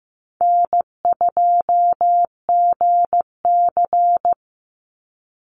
Morse code, heard as a keyed tone, about 15 wpm, 700 Hz. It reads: N2GC
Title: Easy Call Signs - 15wpm